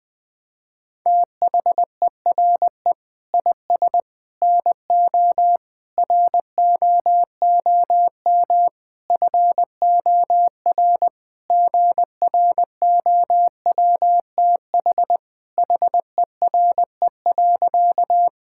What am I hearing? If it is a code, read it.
Morse code, heard as a keyed tone, about 20 wpm, 700 Hz. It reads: THERE IS NO ROOM FOR GROWTH HERE.
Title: Sentences from Top 1000 Words - Review Part 5 - 20wpm